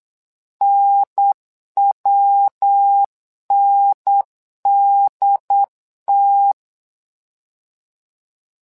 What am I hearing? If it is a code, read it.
Morse code, heard as a keyed tone, about 8 wpm, 800 Hz. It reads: NWNDT
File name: LEZIONI / 3 MWRO / 3 MWRO 8wpm 2606